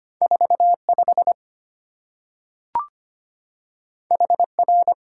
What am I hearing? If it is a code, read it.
Morse code, heard as a keyed tone, about 25 wpm, 700 Hz. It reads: 45 HR
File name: Two Letter-Number Combinations - 25wpm